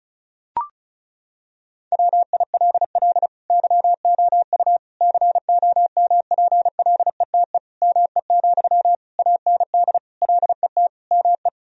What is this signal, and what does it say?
Morse code, heard as a keyed tone, about 36 wpm, 700 Hz. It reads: WILL YOU COMPLETE ME, AND LET ME
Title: Sentences from Top 1000 Words - Review Part 2 - 35wpm